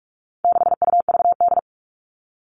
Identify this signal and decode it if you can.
Morse code, heard as a keyed tone, about 44 wpm, 700 Hz. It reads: 6UVD
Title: All Cut Numbers - 4-Digit 45wpm